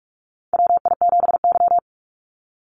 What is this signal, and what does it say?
Morse code, heard as a keyed tone, about 46 wpm, 700 Hz. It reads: WI7Y
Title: USA Call Signs 45wpm